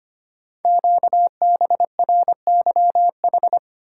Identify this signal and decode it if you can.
Morse code, heard as a keyed tone, about 25 wpm, 700 Hz. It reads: QBRYH
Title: Code Groups - 5 Letters - 25wpm